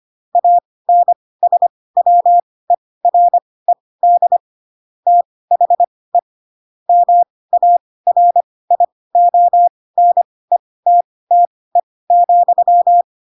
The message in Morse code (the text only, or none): ANSWERED THE MARIONETTE,